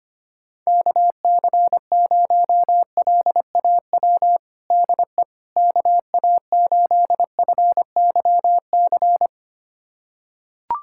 Morse code, heard as a keyed tone, about 25 wpm, 700 Hz. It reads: KC0LAW DE KA8FYC
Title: USA Callsign DE Callsign - 6 Characters - 25wpm